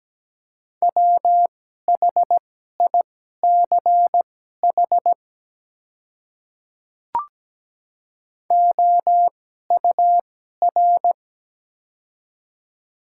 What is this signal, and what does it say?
Morse code, heard as a keyed tone, about 17 wpm, spaced as for 12 wpm, 700 Hz. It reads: WHICH OUR
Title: 2x Character Spacing - Top 200 Words - 17wpm